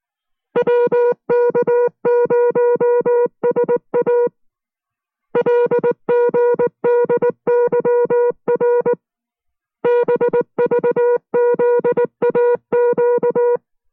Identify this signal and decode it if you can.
Morse code, heard as a keyed tone, about 19 wpm, 480 Hz. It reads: WK0SA LGDYR BVZAQ